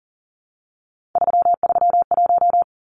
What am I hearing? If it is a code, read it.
Morse code, heard as a keyed tone, about 40 wpm, 700 Hz. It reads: 331
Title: Numbers - 3-Digit 40wpm